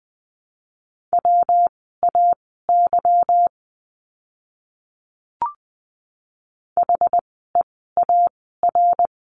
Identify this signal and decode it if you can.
Morse code, heard as a keyed tone, about 20 wpm, spaced as for 14 wpm, 700 Hz. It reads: WAY HEAR
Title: 206 Lesson 33f - Letter j - Words 20wpm